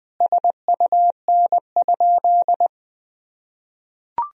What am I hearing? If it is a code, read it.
Morse code, heard as a keyed tone, about 19 wpm, 700 Hz. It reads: SUN?
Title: Sentences from Top 1000 Words - Review Part 2 - 20wpm